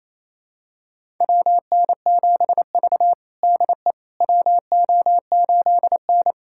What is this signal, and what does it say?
Morse code, heard as a keyed tone, about 28 wpm, 700 Hz. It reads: WN7V DE WO8N
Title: USA Callsign DE Callsign - 4 Characters - 28wpm